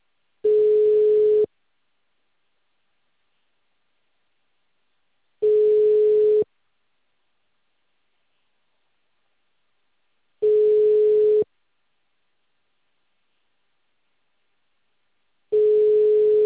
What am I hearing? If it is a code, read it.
Ringback tone (Europe, 425 Hz).
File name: out- -2044- - - .11666